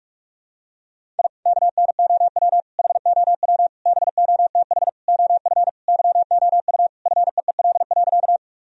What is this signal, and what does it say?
Morse code, heard as a keyed tone, about 42 wpm, 700 Hz. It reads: I KNOW HOW BOTH OF YOU FEEL.
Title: Sentences from Top 1000 Words - Review Part 1 - 45wpm